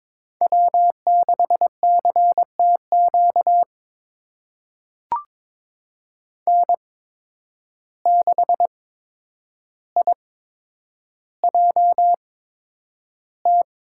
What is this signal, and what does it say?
Morse code, heard as a keyed tone, about 22 wpm, spaced as for 9 wpm, 700 Hz. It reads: W6CTQ N6IJT
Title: USA Call Signs - 5 Characters - 8x Character Spacing - 22wpm